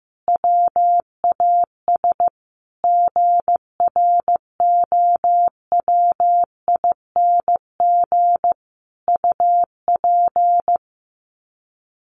Morse code, heard as a keyed tone, about 15 wpm, 700 Hz. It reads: WAS GROWING UP
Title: Sets of 5 Words Encore 15wpm